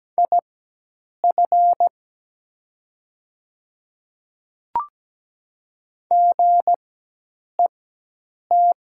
Morse code, heard as a keyed tone, about 17 wpm, spaced as for 8 wpm, 700 Hz. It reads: IF GET
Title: 4x Character Spacing - Top 200 Words - 17wpm